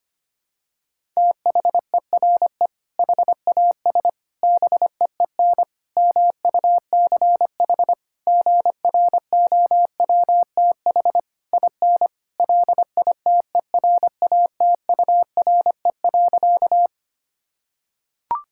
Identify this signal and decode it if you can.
Morse code, heard as a keyed tone, about 25 wpm, 700 Hz. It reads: THERE HAS BEEN MUCH GROWTH IN LITERATURE.
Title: Sentences from Top 1000 Words - Review Part 5 - 25wpm